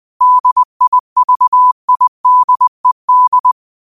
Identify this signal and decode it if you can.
Morse code, heard as a keyed tone, about 20 wpm, 1000 Hz. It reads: DIVIDED
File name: qotdfiles / 20wpm